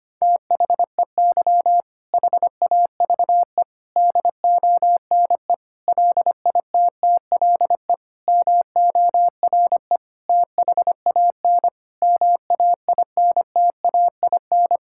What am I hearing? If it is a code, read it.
Morse code, heard as a keyed tone, about 25 wpm, 700 Hz. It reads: THEY HAVE DONE LITTLE MORE THAN MAINTAIN
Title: Sentences from Top 1000 Words - Review Part 5 - 25wpm